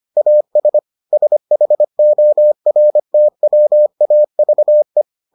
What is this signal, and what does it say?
Morse code, heard as a keyed tone, about 25 wpm, 600 Hz. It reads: AS SHORTWAVE